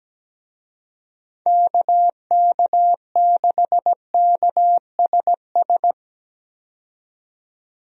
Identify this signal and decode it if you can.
Morse code, heard as a keyed tone, about 17 wpm, 700 Hz. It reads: KK6KSS
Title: USA Call Signs - 6 Characters - 17wpm